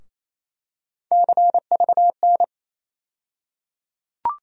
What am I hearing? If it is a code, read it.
Morse code, heard as a keyed tone, about 28 wpm, 700 Hz. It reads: CVN
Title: USA Call Sign Postfixes - 3 Letters 28wpm